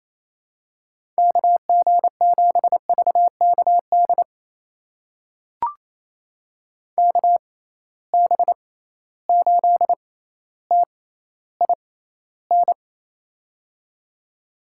Morse code, heard as a keyed tone, about 28 wpm, spaced as for 14 wpm, 700 Hz. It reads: KG7VKD KB8TIN
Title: USA Call Signs - 6 Characters - 6x Character Spacing - 28wpm